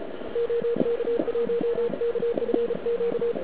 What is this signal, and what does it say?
Morse code, heard as a keyed tone, about 33 wpm, 470 Hz, with radio noise. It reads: OK1KFQ
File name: ARCHIV / KX3 / CW pileup